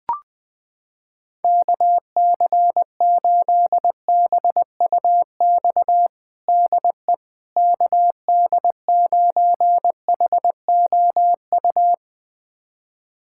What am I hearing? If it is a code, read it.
Morse code, heard as a keyed tone, about 20 wpm, 700 Hz. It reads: KC8BUX DE KD9HOU
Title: USA Callsign DE Callsign - 6 Characters - 20wpm